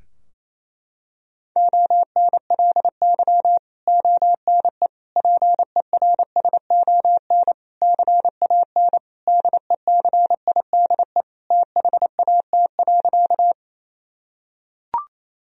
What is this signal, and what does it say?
Morse code, heard as a keyed tone, about 28 wpm, 700 Hz. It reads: ONLY ONE PERSON CAN DECIDE THAT.
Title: Sentences from Top 1000 Words - Review Part 3 - 28wpm